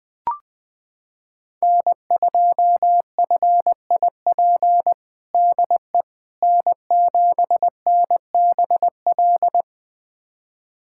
Morse code, heard as a keyed tone, about 20 wpm, 700 Hz. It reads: N2FIP DE N7NBL
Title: USA Callsign DE Callsign - 5 Characters - 20wpm